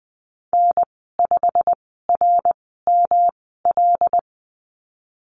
Morse code, heard as a keyed tone, about 20 wpm, spaced as for 15 wpm, 700 Hz. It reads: N5RML
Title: 140 Lesson 23g - Letter f - Call Signs 20wpm